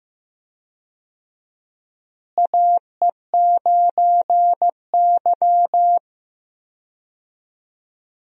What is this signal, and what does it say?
Morse code, heard as a keyed tone, about 15 wpm, 700 Hz. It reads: AE9Y
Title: USA Call Signs - 4 Characters - Repeated 2x - 15wpm